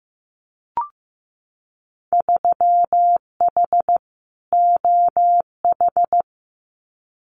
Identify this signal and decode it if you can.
Morse code, heard as a keyed tone, about 15 wpm, 700 Hz. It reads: 3H OH
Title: Winter Field Day Exchanges Encore 15wpm